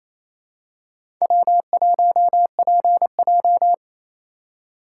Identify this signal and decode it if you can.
Morse code, heard as a keyed tone, about 28 wpm, 700 Hz. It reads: W1PJ
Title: USA Call Signs - 4 Characters - 6x Character Spacing - 28wpm